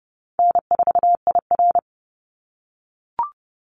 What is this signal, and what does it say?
Morse code, heard as a keyed tone, about 30 wpm, 700 Hz. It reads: N4IR
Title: Easy Call Signs - 30wpm